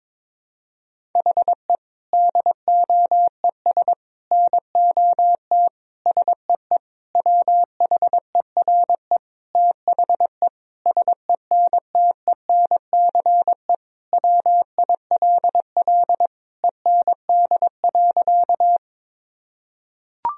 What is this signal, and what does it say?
Morse code, heard as a keyed tone, about 22 wpm, 700 Hz. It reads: HE DOES NOT SEE WHERE THE SENTENCE WILL END.
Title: Sentences from Top 1000 Words - Review Part 1 - 22wpm